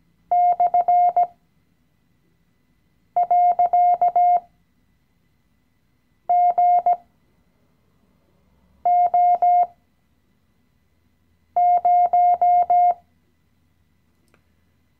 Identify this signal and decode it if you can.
Morse code, heard as a keyed tone, about 17 wpm, spaced as for 4 wpm, 710 Hz. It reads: /.GO0
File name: CW-Online-Kurs / Anfänger / Lektion 016